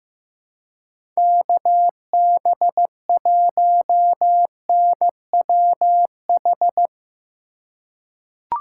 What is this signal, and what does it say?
Morse code, heard as a keyed tone, about 15 wpm, 700 Hz. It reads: KB1NWH
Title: USA Call Signs - 2x Character Spacing - 15wpm